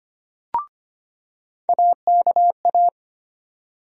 Morse code, heard as a keyed tone, about 25 wpm, 700 Hz. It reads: AKA
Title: USA Call Sign Postfixes - 3 Letters 25wpm